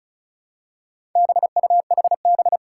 Morse code, heard as a keyed tone, about 36 wpm, 700 Hz. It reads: BUHB
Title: Four Challenging Letter-Number Combinations 35wpm